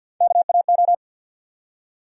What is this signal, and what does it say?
Morse code, heard as a keyed tone, about 52 wpm, 700 Hz. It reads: KAO
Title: USA Call Sign Postfixes - 3 Letters 50wpm